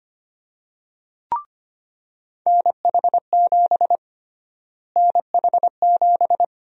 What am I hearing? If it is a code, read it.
Morse code, heard as a keyed tone, about 25 wpm, 700 Hz. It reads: NH7 NH7
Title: USA Call Sign Prefixes - Repeated 2x - 25wpm